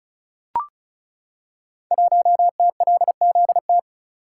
Morse code, heard as a keyed tone, about 36 wpm, 700 Hz. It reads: 1TLZT
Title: Code Groups - 5 Letter:Numbers - 35wpm